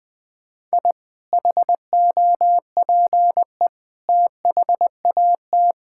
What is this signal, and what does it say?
Morse code, heard as a keyed tone, about 20 wpm, 700 Hz. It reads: I HOPE THAT
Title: Sentences from Top 1000 Words - Review Part 5 - 20wpm